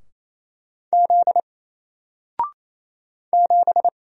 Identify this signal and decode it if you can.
Morse code, heard as a keyed tone, about 27 wpm, 700 Hz. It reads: Z 7
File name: Single-Letter-Number-28wpm